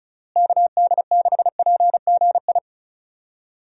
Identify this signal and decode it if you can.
Morse code, heard as a keyed tone, about 34 wpm, 700 Hz. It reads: KD6PGI
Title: USA Call Signs - 6 Characters - Repeated 3x - 35wpm